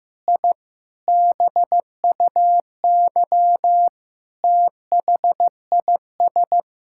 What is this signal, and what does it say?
Morse code, heard as a keyed tone, about 16 wpm, 700 Hz. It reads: I BUY THIS
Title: Sentences from Top 1000 Words - Review Part 5 - 15wpm